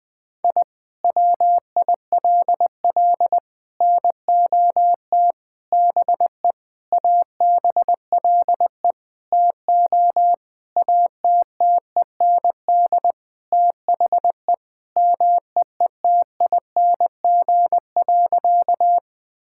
Morse code, heard as a keyed tone, about 20 wpm, 700 Hz. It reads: I WILL NOT BE ABLE TO ATTEND THE MEETING.
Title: Sentences with 5-Word Phrases - 20wpm